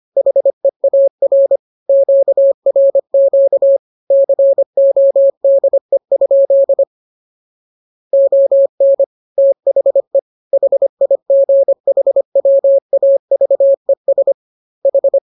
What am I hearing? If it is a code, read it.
Morse code, heard as a keyed tone, about 25 wpm, 550 Hz. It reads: HEAR QRQ CODE? ON THE HIGHWAVES H
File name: jokes / joke 10 25 0000